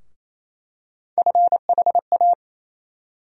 Morse code, heard as a keyed tone, about 28 wpm, 700 Hz. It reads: FHA
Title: USA Call Sign Postfixes - 3 Letters 28wpm